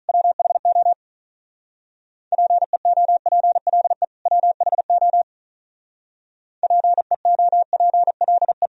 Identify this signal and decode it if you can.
Morse code, heard as a keyed tone, about 46 wpm, 700 Hz. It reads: WHO PEOPLE WHO PEOPLE